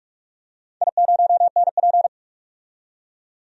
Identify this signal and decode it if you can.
Morse code, heard as a keyed tone, about 44 wpm, 700 Hz. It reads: I0NP
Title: DXCC-100 Call Signs - 4 Characters - 45wpm